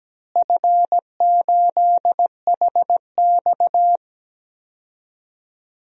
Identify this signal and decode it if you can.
Morse code, heard as a keyed tone, about 17 wpm, 700 Hz. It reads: F8HX
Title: DXCC-100 Call Signs - 4 Characters - Repeated 2x - 17wpm